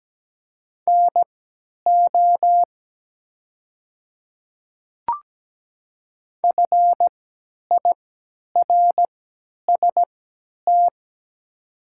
Morse code, heard as a keyed tone, about 17 wpm, spaced as for 10 wpm, 700 Hz. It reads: NO FIRST
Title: 3x Character Spacing - Top 200 Words - 17wpm